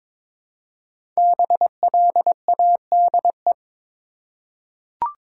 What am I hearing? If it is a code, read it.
Morse code, heard as a keyed tone, about 22 wpm, 700 Hz. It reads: BLADE
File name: 5 Letter Words - First 3 Letters Given - 22wpm